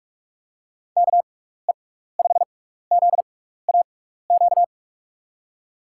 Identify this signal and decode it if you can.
Morse code, heard as a keyed tone, about 44 wpm, spaced as for 15 wpm, 700 Hz. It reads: KE5ZAQ
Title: USA Call Signs - 6x Character Spacing - 45wpm